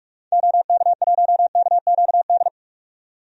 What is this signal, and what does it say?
Morse code, heard as a keyed tone, about 46 wpm, 700 Hz. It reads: OK1KQD
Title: DXCC-100 Call Signs - 6 Characters - 45wpm